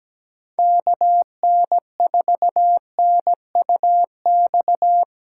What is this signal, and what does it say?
Morse code, heard as a keyed tone, about 20 wpm, 700 Hz. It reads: KN4NUX
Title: Speed Racing - USA 2x3 Call Signs 17wpm